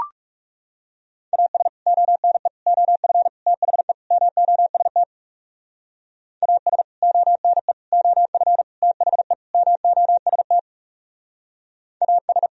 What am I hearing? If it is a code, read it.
Morse code, heard as a keyed tone, about 38 wpm, 700 Hz. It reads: AS ONE OF THE MOST AS ONE OF THE MOST AS
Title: Speed-Racing - Sets of 5 Words 30wpm